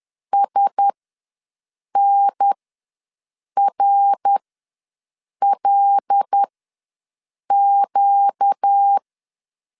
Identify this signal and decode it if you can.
Morse code, heard as a keyed tone, about 10 wpm, spaced as for 6 wpm, 800 Hz. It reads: SNRLQ
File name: LEZIONI / 5 LBVC / 5 L ws 10wpm 3636